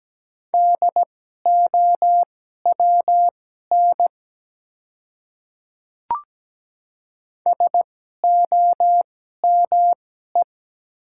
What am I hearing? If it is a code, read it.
Morse code, heard as a keyed tone, about 17 wpm, spaced as for 12 wpm, 700 Hz. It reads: DOWN SOME
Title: 2x Character Spacing - Top 100 Words - 17wpm